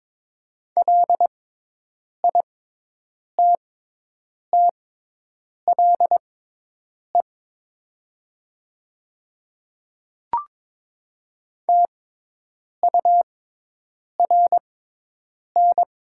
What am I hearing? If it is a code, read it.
Morse code, heard as a keyed tone, about 22 wpm, spaced as for 8 wpm, 700 Hz. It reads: LITTLE TURN
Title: 6x Character Spacing - Top 200 Words - 22wpm